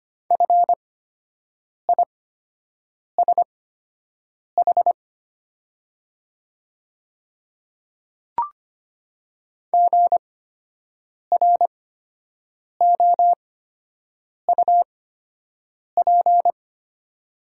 Morse code, heard as a keyed tone, about 25 wpm, spaced as for 7 wpm, 700 Hz. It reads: FISH GROUP
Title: 8x Character Spacing - Top 300 Words - 25wpm